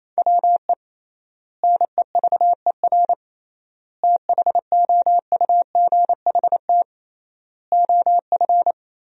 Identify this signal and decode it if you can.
Morse code, heard as a keyed tone, about 28 wpm, 700 Hz. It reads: WE NEVER THOUGHT OF
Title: 3x Word-Spacing - Sentences from Top 300 Words - 28wpm